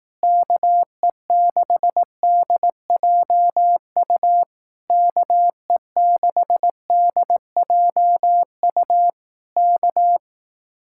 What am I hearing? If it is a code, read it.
Morse code, heard as a keyed tone, about 18 wpm, 700 Hz. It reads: KE6DJU KE6DJU K
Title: Speed-Racing Common QSO Phrases Encore - 15wpm